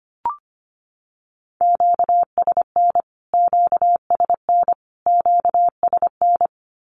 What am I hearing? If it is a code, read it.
Morse code, heard as a keyed tone, about 25 wpm, 700 Hz. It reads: QSN QSN QSN
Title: Q-Codes Repeated 3x 25wpm